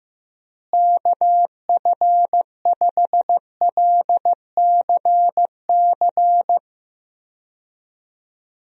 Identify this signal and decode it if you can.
Morse code, heard as a keyed tone, about 15 wpm, 700 Hz. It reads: KF5LCC
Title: USA Call Signs - 6 Characters - Repeated 2x - 15wpm